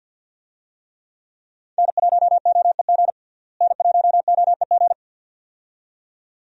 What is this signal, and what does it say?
Morse code, heard as a keyed tone, about 52 wpm, 700 Hz. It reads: N1OEG N1OEG
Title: USA Call Signs - 5 Characters - Repeated 2x - 50wpm